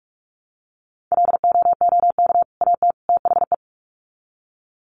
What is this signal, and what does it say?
Morse code, heard as a keyed tone, about 44 wpm, 700 Hz. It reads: LOOK AT THE
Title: Sets of 3 Words 45wpm